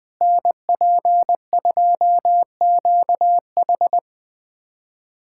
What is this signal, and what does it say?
Morse code, heard as a keyed tone, about 20 wpm, 700 Hz. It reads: NP2QH
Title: USA Callsign DE Callsign - 5 Characters - 20wpm